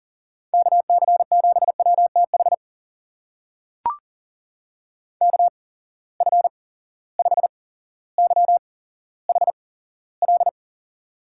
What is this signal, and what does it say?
Morse code, heard as a keyed tone, about 40 wpm, spaced as for 17 wpm, 700 Hz. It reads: KC7WTH KF5YHL
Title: USA Call Signs - 6 Characters - 8x Character Spacing - 40wpm